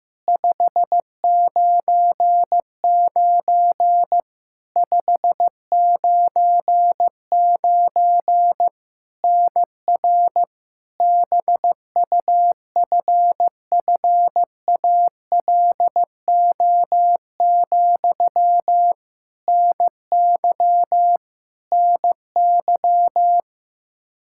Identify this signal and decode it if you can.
Morse code, heard as a keyed tone, about 15 wpm, 700 Hz. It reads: 599 599 NR BUFFALO, NY NY
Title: K1USN SST QSO Practice 15wpm